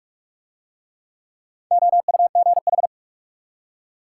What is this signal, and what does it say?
Morse code, heard as a keyed tone, about 45 wpm, 700 Hz. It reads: OUGH